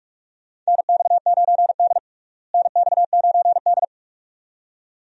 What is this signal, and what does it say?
Morse code, heard as a keyed tone, about 46 wpm, 700 Hz. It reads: NX9D NX9D
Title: USA Call Signs - 4 Characters - Repeated 2x - 45wpm